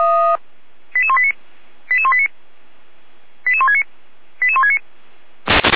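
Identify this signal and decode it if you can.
Five-tone selective call (ZVEI), heard as quick runs of five tones, tones of about 70 ms. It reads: 88189 88189 88169 88169
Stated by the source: ZVEI five-tone call 88189 received by SDR.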